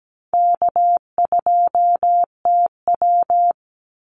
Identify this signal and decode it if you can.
Morse code, heard as a keyed tone, about 17 wpm, 700 Hz. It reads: K2TW
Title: Easy Call Signs Repeated 3x - 17wpm